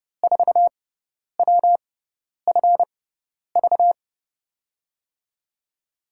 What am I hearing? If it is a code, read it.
Morse code, heard as a keyed tone, about 30 wpm, spaced as for 10 wpm, 700 Hz. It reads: 4WFV
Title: USA Call Signs - 5 Characters - 6x Character Spacing - 30wpm